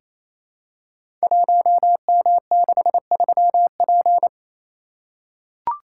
Morse code, heard as a keyed tone, about 27 wpm, 700 Hz. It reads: 1M63P
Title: Code Groups - 5 Letter:Numbers - 28wpm